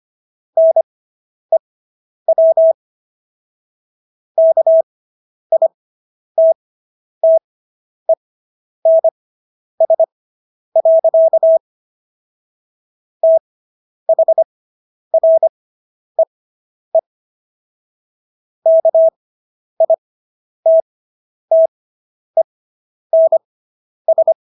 Morse code, heard as a keyed tone, about 25 wpm, spaced as for 10 wpm, 650 Hz. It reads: NEW KITTENS. THREE KITTENS